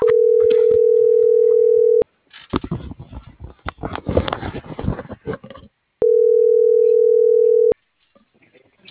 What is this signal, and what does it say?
Ringback tone (North America), 440 + 480 Hz.